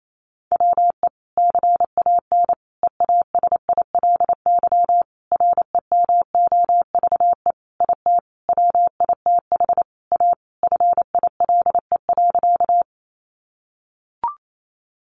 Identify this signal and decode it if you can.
Morse code, heard as a keyed tone, about 28 wpm, 700 Hz. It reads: WE CAN EASILY REMOVE IT WITH A FILE.
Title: Sentences from Top 1200 Words 28wpm